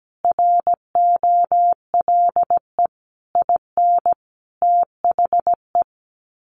Morse code, heard as a keyed tone, about 17 wpm, 700 Hz. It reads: ROLE IN THE
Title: Sets of 5 Words Encore2 17wpm